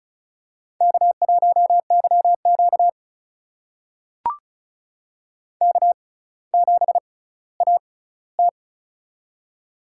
Morse code, heard as a keyed tone, about 36 wpm, spaced as for 18 wpm, 700 Hz. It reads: K1YQ K7AT
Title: USA Call Signs - 4 Characters - 6x Character Spacing - 35wpm